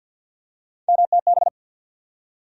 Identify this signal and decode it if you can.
Morse code, heard as a keyed tone, about 53 wpm, 700 Hz. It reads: MTB
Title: USA Call Sign Postfixes - 3 Letters 50wpm